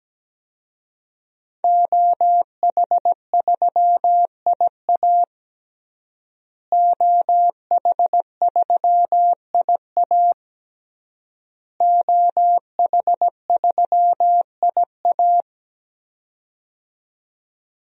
Morse code, heard as a keyed tone, about 17 wpm, 700 Hz. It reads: OH3IA OH3IA OH3IA
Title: DXCC-100 Call Signs - 5 Characters - Repeated 3x - 17wpm